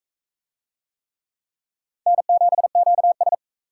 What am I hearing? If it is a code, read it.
Morse code, heard as a keyed tone, about 38 wpm, 700 Hz. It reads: N7QS